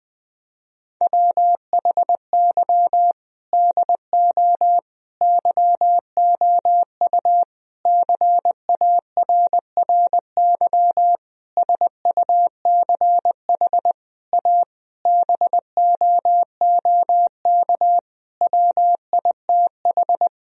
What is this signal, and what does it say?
Morse code, heard as a keyed tone, about 20 wpm, 700 Hz. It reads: WHY DO YOU CARRY SUCH A BOOK WITH
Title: Sentences from Top 1000 Words - Review Part 1 - 20wpm